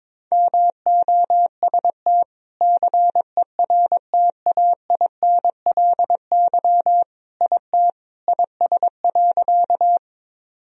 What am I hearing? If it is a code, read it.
Morse code, heard as a keyed tone, about 21 wpm, 700 Hz. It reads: MOST CERTAINLY IT IS.
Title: Sentences from Top 1000 Words - Review Part 1 - 22wpm